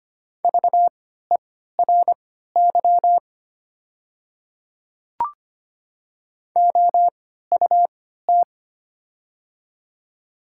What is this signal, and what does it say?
Morse code, heard as a keyed tone, about 25 wpm, spaced as for 14 wpm, 690 Hz. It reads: VERY OUT
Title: 3x Character Spacing - Top 200 Words - 25wpm